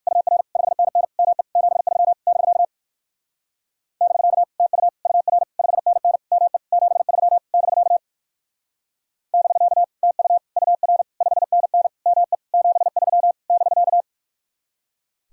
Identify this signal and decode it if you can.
Morse code, heard as a keyed tone, about 58 wpm, 700 Hz. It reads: UR 5NN ME 73 <BK> <BK> TU UR 5NN ME 73 <BK> <BK> TU UR 5NN ME 73 <BK>